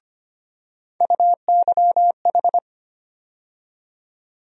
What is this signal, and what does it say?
Morse code, heard as a keyed tone, about 25 wpm, 700 Hz. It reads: UYH
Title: USA Call Sign Postfixes - 3 Letters 25wpm